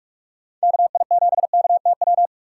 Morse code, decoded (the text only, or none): KI7KTW